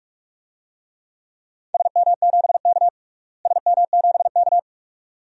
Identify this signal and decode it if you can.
Morse code, heard as a keyed tone, about 43 wpm, 700 Hz. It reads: SM7K SM7K
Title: DXCC-100 Call Signs - 4 Characters - Repeated 2x - 45wpm